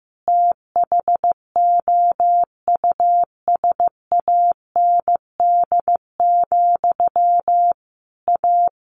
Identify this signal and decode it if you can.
Morse code, heard as a keyed tone, about 15 wpm, 700 Hz. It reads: THOUSAND, A
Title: A Sound Of Thunder 15wpm